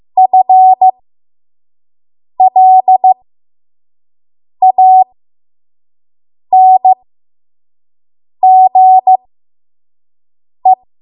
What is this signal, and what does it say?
Morse code, heard as a keyed tone, about 15 wpm, spaced as for 5 wpm, 750 Hz. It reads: FLANGE
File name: Morse / 05WPM